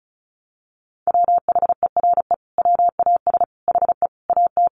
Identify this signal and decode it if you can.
Morse code, heard as a keyed tone, about 34 wpm, 700 Hz. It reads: WHERE WAS HE AT
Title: Sentences from Top 1000 Words 35wpm